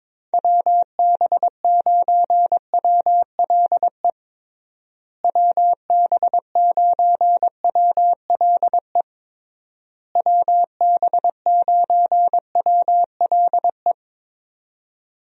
Morse code, heard as a keyed tone, about 22 wpm, 700 Hz. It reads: WB9WLE WB9WLE WB9WLE
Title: USA Call Signs - 6 Characters - Repeated 3x - 22wpm